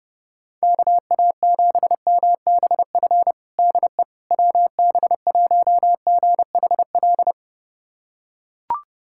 Morse code, heard as a keyed tone, about 30 wpm, 700 Hz. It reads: KA7MBF DE WB1GHL
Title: USA Callsign DE Callsign - 6 Characters - 30wpm